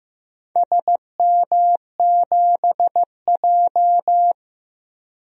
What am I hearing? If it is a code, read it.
Morse code, heard as a keyed tone, about 15 wpm, 700 Hz. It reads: SM7J
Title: DXCC-100 Call Signs - 4 Characters - 15wpm